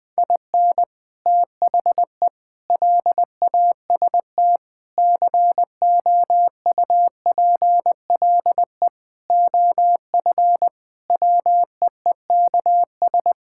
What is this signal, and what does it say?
Morse code, heard as a keyed tone, about 20 wpm, 700 Hz. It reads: IN THE LAST COUPLE OF WEEKS
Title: Sentences with 5-Word Phrases - 20wpm